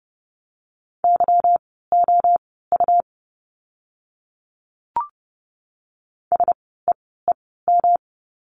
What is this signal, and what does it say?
Morse code, heard as a keyed tone, about 30 wpm, spaced as for 17 wpm, 700 Hz. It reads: YOU SEEM
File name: course / 30:10wpm / 237 Lesson 38f - Letter q - Words 30wpm